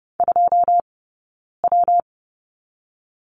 Morse code, heard as a keyed tone, about 31 wpm, 700 Hz. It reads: 2 W
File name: Lesson 19d - Letter w - Two Characters 30wpm